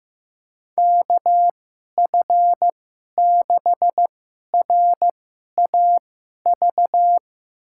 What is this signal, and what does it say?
Morse code, heard as a keyed tone, about 16 wpm, spaced as for 11 wpm, 700 Hz. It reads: KF6RAV
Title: USA Call Signs - 6 Characters - 2x Character Spacing - 15wpm